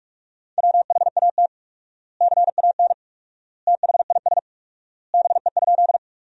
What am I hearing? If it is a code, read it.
Morse code, heard as a keyed tone, about 42 wpm, 700 Hz. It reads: WHAT CAN THIS BE?
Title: 4x Word-Spacing - Sentences from Top 100 Words - 45wpm